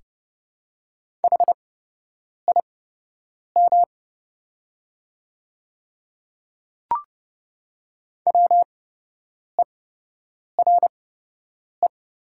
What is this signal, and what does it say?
Morse code, heard as a keyed tone, about 30 wpm, spaced as for 8 wpm, 700 Hz. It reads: HIM WERE
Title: 8x Character Spacing - Top 100 Words - 30wpm